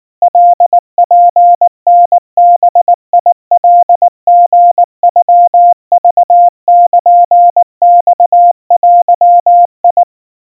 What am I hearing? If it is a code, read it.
Morse code (Japanese Wabun code), heard as a keyed tone, about 19 wpm, 700 Hz. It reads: カツタバカリノクルマデ
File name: kawa / Morse sample / kamyu 70 cpm 700 Hz Crispy